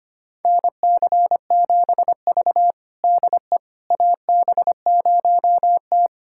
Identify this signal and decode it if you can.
Morse code, heard as a keyed tone, about 25 wpm, 700 Hz. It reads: NC7V DE AB0T
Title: USA Callsign DE Callsign - 4 Characters - 25wpm